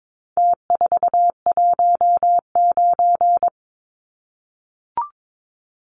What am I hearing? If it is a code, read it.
Morse code, heard as a keyed tone, about 22 wpm, 700 Hz. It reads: T419
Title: Numbers - Contest Style - 4-Digit 22wpm